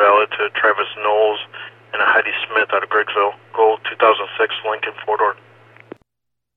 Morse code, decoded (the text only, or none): none